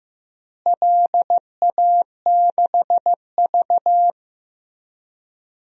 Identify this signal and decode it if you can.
Morse code, heard as a keyed tone, about 15 wpm, 700 Hz. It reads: LA6V
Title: DXCC-100 Call Signs - 4 Characters - Repeated 2x - 15wpm